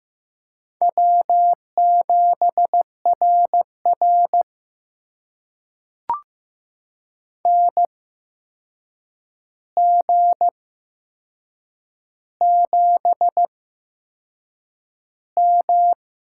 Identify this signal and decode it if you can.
Morse code, heard as a keyed tone, about 15 wpm, spaced as for 6 wpm, 700 Hz. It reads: W7RR NG7M
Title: USA Call Signs - 4 Characters - 8x Character Spacing - 15wpm